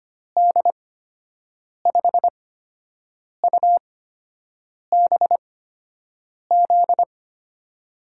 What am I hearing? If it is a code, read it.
Morse code, heard as a keyed tone, about 25 wpm, spaced as for 7 wpm, 700 Hz. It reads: D5UBZ
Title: USA Call Signs - 8x Character Spacing - 25wpm